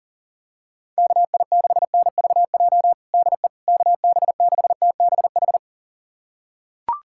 Morse code, heard as a keyed tone, about 40 wpm, 700 Hz. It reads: KI6NVJ DE KB6TBH
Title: USA Callsign DE Callsign - 6 Characters - 40wpm